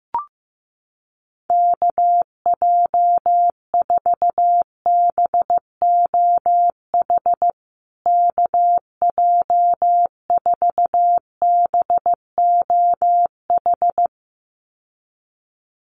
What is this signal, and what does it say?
Morse code, heard as a keyed tone, about 15 wpm, 700 Hz. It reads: KJ4BOH KJ4BOH
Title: USA Call Signs Repeated 2x 15wpm